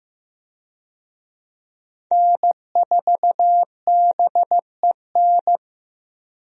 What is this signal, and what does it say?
Morse code, heard as a keyed tone, about 15 wpm, 700 Hz. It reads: N4BEN
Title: USA Call Signs - 5 Characters - 15wpm